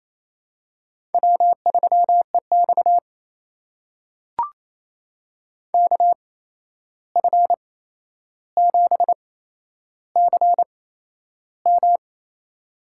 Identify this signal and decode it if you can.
Morse code, heard as a keyed tone, about 28 wpm, spaced as for 11 wpm, 700 Hz. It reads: W3EX KF7CM
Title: USA Call Signs - 8x Character Spacing - 28wpm